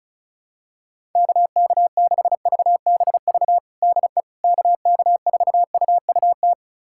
Morse code, heard as a keyed tone, about 36 wpm, 700 Hz. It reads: KK6VBV DE KK4UUT
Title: USA Callsign DE Callsign - 6 Characters - 35wpm